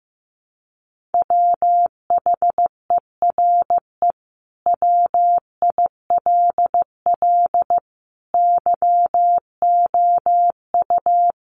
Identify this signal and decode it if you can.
Morse code, heard as a keyed tone, about 14 wpm, 700 Hz. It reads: WHERE WILL YOU
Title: Sentences from Top 1300 Words 15wpm